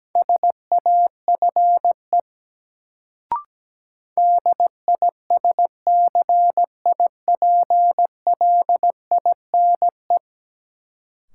Morse code, heard as a keyed tone, about 17 wpm, 700 Hz. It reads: SAFE DISCIPLINE
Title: Sentences from Top 900 Words - New Word Drill - 17wpm